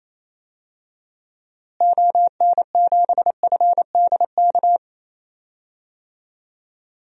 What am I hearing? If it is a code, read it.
Morse code, heard as a keyed tone, about 28 wpm, 700 Hz. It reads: ON7FDK
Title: DXCC-100 Call Signs - 6 Characters - Repeated 3x - 28wpm